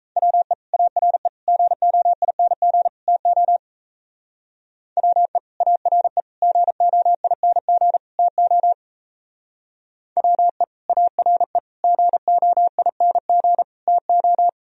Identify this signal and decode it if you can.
Morse code, heard as a keyed tone, about 33 wpm, 700 Hz. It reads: WE ARE GOING TO WE ARE GOING TO WE ARE GOING TO
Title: Speed-Racing - Sets of 4 Words 28wpm